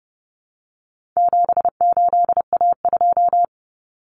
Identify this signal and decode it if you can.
Morse code, heard as a keyed tone, about 30 wpm, 700 Hz. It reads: 78A2
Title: Common Cut Numbers - 4-Digit 30wpm